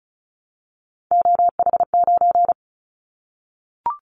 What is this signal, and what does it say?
Morse code, heard as a keyed tone, about 36 wpm, 700 Hz. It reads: OH9
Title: Callsign Prefixes for Top 50 DXCC - 35wpm